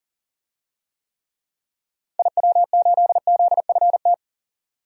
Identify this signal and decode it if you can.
Morse code, heard as a keyed tone, about 38 wpm, 700 Hz. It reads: IW8ZFT
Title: DXCC-100 Call Signs - 6 Characters - 40wpm